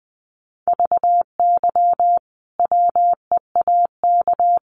Morse code, heard as a keyed tone, about 20 wpm, 700 Hz. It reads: VY WEAK
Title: Common QSO Phrases 20wpm